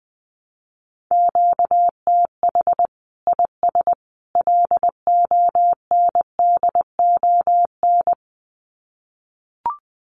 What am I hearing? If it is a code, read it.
Morse code, heard as a keyed tone, about 20 wpm, 700 Hz. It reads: QTH IS LONDON
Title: Common QSO Phrases 20wpm